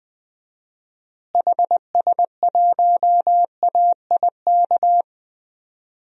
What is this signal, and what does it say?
Morse code, heard as a keyed tone, about 21 wpm, 700 Hz. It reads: HS1AIK
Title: DXCC-100 Call Signs - 6 Characters - Repeated 2x - 20wpm